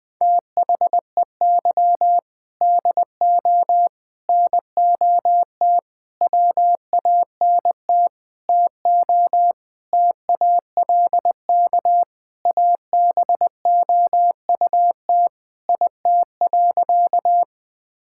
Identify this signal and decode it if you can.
Morse code, heard as a keyed tone, about 21 wpm, 700 Hz. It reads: THEY DO NOT WANT TO TALK ABOUT IT.
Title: Sentences with 5-Word Phrases - 20wpm